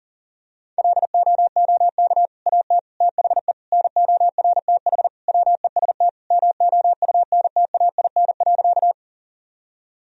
Morse code, heard as a keyed tone, about 40 wpm, 700 Hz. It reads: LOOK AT THE NORTH WEST MOUNTAIN.
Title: Sentences from Top 1000 Words - Review Part 2 - 40wpm